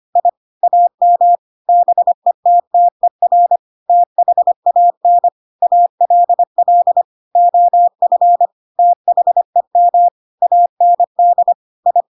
Morse code, heard as a keyed tone, about 25 wpm, 700 Hz. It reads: I AM BETTER THAN ALL OF THEM AND I
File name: Pinocchio / Pinocchio ch8 25 25 ik1hge